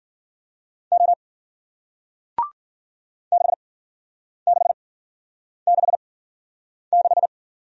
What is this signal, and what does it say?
Morse code, heard as a keyed tone, about 54 wpm, 700 Hz. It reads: O 6 6 6 6